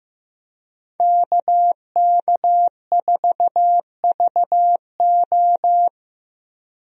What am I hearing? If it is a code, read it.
Morse code, heard as a keyed tone, about 15 wpm, 700 Hz. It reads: KK4VO
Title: USA Call Signs - 5 Characters - Repeated 2x - 15wpm